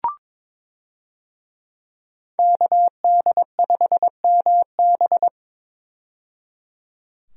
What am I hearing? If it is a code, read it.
Morse code, heard as a keyed tone, about 22 wpm, 700 Hz. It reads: KD5MB
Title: USA Call Signs - 5 Characters - 22wpm